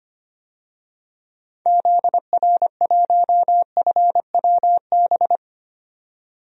Morse code, heard as a keyed tone, about 25 wpm, 700 Hz. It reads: ZR1FWB
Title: DXCC-100 Call Signs - 6 Characters - 25wpm